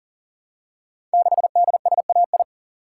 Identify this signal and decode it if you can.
Morse code, heard as a keyed tone, about 42 wpm, 700 Hz. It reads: 6DSAI
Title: Code Groups - 5 Letter:Numbers - 40wpm